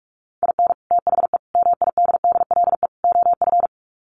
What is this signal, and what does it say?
Morse code, heard as a keyed tone, about 45 wpm, 700 Hz. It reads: IN THE MIDDLE OF
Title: Sets of 4 Words 45wpm